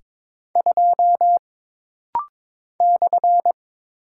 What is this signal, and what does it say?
Morse code, heard as a keyed tone, about 22 wpm, 700 Hz. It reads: 2 /
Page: This rapid-fire timing is geared to push your ICR (Instant Character Recognition) skills.